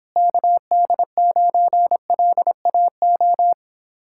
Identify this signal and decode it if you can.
Morse code, heard as a keyed tone, about 26 wpm, 700 Hz. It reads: KD9LAO